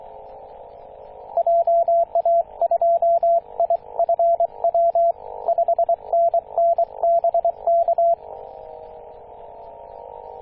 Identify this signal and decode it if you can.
Morse code, heard as a keyed tone, about 23 wpm, 680 Hz, with radio noise. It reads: JA2IFW 5NN BK
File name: posts / 2016-june-26-0520z-ac2cx-kh2-21-027mhz-cw